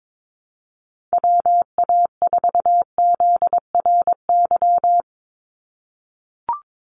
Morse code, heard as a keyed tone, about 22 wpm, 700 Hz. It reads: WA4ZRY
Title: USA Call Signs 22wpm